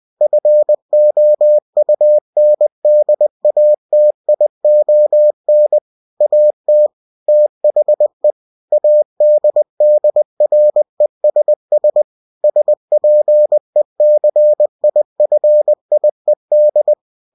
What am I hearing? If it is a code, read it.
Morse code, heard as a keyed tone, about 20 wpm, 600 Hz. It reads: FOUNDATION AT THE ADDRESS SPECIFIED
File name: cw-books / winnie-the-pooh / winnie-20wpm-page-0117